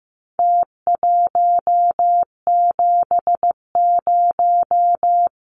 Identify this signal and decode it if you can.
Morse code, heard as a keyed tone, about 15 wpm, 700 Hz. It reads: T170
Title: Numbers - Contest Style - 4-Digit 15wpm